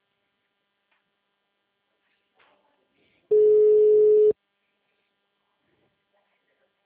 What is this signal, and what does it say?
Ringback tone (Europe, 425 Hz), with speech.